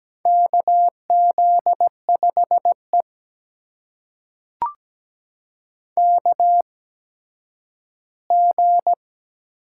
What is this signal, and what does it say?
Morse code, heard as a keyed tone, about 17 wpm, spaced as for 10 wpm, 700 Hz. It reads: KZ5E KG
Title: USA Call Signs - 8x Character Spacing - 17wpm